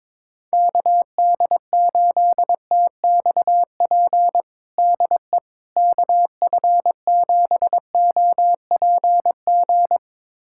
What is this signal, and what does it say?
Morse code, heard as a keyed tone, about 22 wpm, 700 Hz. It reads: KD8TXP DE KF7OPG
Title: USA Callsign DE Callsign - 6 Characters - 22wpm